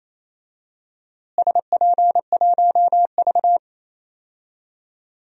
Morse code, heard as a keyed tone, about 28 wpm, 700 Hz. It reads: SP1V
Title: DXCC-100 Call Signs - 4 Characters - Repeated 3x - 28wpm